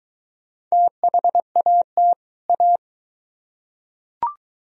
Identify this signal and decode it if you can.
Morse code, heard as a keyed tone, about 22 wpm, 700 Hz. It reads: THAT A